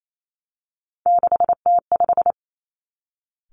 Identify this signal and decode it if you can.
Morse code, heard as a keyed tone, about 28 wpm, 700 Hz. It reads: 6T5
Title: Three Challenging Letter-Number Combinations 28wpm